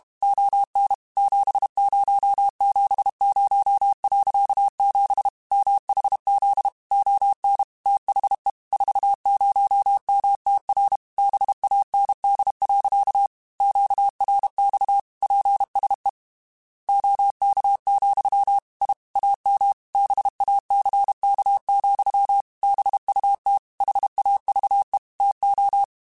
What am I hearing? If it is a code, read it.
Morse code, heard as a keyed tone, about 32 wpm, 800 Hz. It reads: ON 7070.7 MHZ ON THE 40MTR BAND. QRX PSE OK, I AM BACK, BUT HAVE TO
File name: QSOs / QSOs-mp3s / kc7haa@30